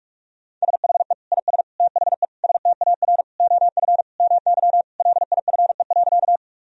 Morse code, heard as a keyed tone, about 46 wpm, 700 Hz. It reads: SHE IS THE STAR OF MY LIFE.
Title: Sentences from Top 1000 Words - Review Part 2 - 45wpm